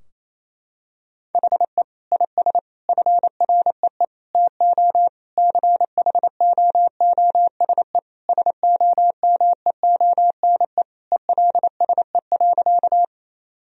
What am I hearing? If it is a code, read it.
Morse code, heard as a keyed tone, about 28 wpm, 700 Hz. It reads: HE IS FREE TO CHOOSE SOMEONE ELSE.
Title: Sentences from Top 1000 Words - Review Part 4 - 28wpm